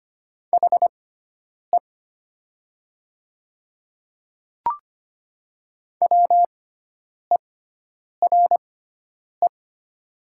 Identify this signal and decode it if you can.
Morse code, heard as a keyed tone, about 25 wpm, spaced as for 9 wpm, 690 Hz. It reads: HE WERE
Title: 6x Character Spacing - Top 200 Words - 25wpm